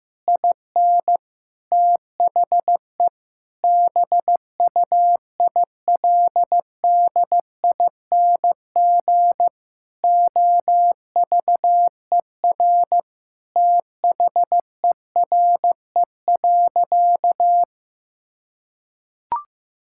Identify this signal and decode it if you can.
Morse code, heard as a keyed tone, about 15 wpm, 700 Hz. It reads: IN THE BUILDING OVER THERE.
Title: Sentences from Top 1000 Words - Review Part 5 - 15wpm